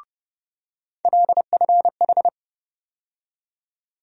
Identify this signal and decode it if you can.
Morse code, heard as a keyed tone, about 32 wpm, 700 Hz. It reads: LFH
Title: USA Call Sign Postfixes - 3 Letters 30wpm